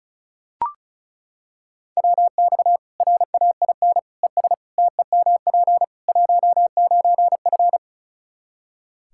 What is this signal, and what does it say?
Morse code, heard as a keyed tone, about 34 wpm, 700 Hz. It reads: WX RAIN ES TEMP 19F
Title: Practice QSO - Exchange 2 - Info - 35wpm